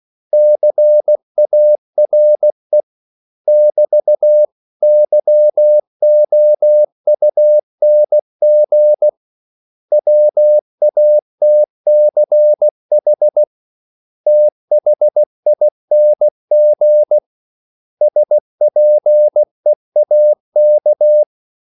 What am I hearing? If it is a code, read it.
Morse code, heard as a keyed tone, about 16 wpm, 600 Hz. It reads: CARE <BT> YOUNG WATCH THING SPEAK
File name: CW / Practice / 500MostCommonEnglishWords16